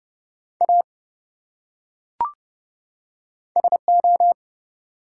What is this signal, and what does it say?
Morse code, heard as a keyed tone, about 28 wpm, 700 Hz. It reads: A SO
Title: Top 180 QSO Elements 30wpm